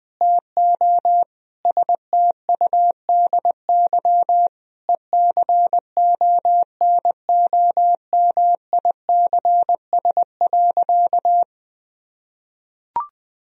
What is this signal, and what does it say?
Morse code, heard as a keyed tone, about 20 wpm, 700 Hz. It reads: TO STUDY ECONOMICS.
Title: Sentences from Top 1000 Words - Review Part 3 - 20wpm